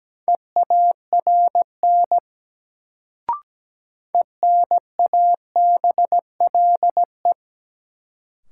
Morse code, heard as a keyed tone, about 17 wpm, 700 Hz. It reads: EARN ENABLE
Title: Sentences from Top 1500 Words - New Word Drill Alt - 17wpm